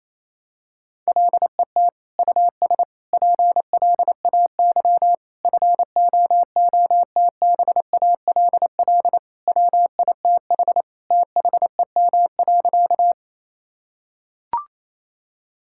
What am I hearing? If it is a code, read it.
Morse code, heard as a keyed tone, about 28 wpm, 700 Hz. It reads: LET US PLAY FOOTBALL WITH THEM.
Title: Sentences from Top 1600 Words 28wpm